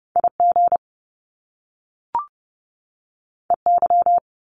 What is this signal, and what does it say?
Morse code, heard as a keyed tone, about 30 wpm, 700 Hz. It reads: IG EY
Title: Top Two Letter Combinations in Words 30wpm